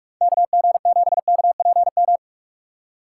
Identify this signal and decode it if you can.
Morse code, heard as a keyed tone, about 42 wpm, 700 Hz. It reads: KG7KPM
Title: USA Call Signs - 6 Characters - 4x Character Spacing - 45wpm